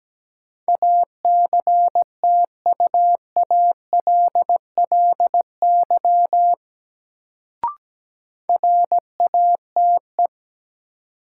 Morse code, heard as a keyed tone, about 17 wpm, 700 Hz. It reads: ACTUALLY RATE
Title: Sentences from Top 700 Words - New Word Drill - 17wpm